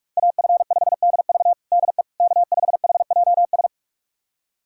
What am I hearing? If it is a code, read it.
Morse code, heard as a keyed tone, about 47 wpm, 700 Hz. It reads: AF5DV DE K5HJS
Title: USA Callsign DE Callsign - 5 Characters - 45wpm